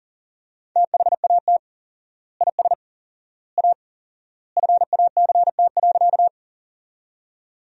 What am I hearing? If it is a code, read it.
Morse code, heard as a keyed tone, about 40 wpm, 700 Hz. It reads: THAT IS A FACT.
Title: 4x Word-Spacing - Sentences from Top 500 Words - 40wpm